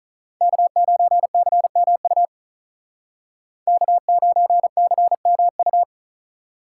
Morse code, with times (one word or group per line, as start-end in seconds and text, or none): 0.4-2.3 K9CMU
3.7-5.8 K9CMU